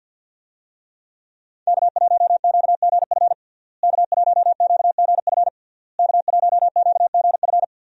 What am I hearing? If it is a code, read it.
Morse code, heard as a keyed tone, about 50 wpm, 700 Hz. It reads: K1QGF K1QGF K1QGF
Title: USA Call Signs - 5 Characters - Repeated 3x - 50wpm